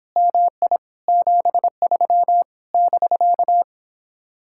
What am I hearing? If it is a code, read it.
Morse code, heard as a keyed tone, about 29 wpm, 700 Hz. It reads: MI 73 <BK>